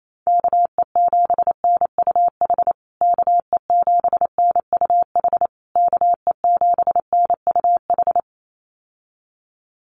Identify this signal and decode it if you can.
Morse code, heard as a keyed tone, about 28 wpm, 700 Hz. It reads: KE7NUH KE7NUH KE7NUH
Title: USA Call Signs Repeated 3x 28wpm